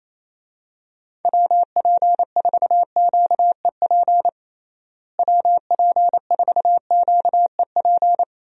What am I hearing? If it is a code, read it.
Morse code, heard as a keyed tone, about 28 wpm, 700 Hz. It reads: WP4QEP WP4QEP
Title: USA Call Signs - 6 Characters - Repeated 3x - 28wpm